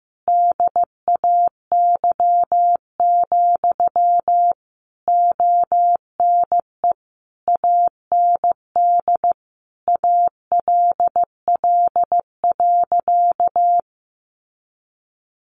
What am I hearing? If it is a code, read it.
Morse code, heard as a keyed tone, about 15 wpm, 700 Hz. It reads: DAY, ONE AND ALL.
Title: Sentences from Top 200 Words 15wpm